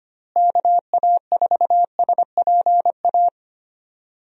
Morse code, heard as a keyed tone, about 25 wpm, 700 Hz. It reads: KA4SPA
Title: USA Call Signs - 6 Characters - Repeated 2x - 25wpm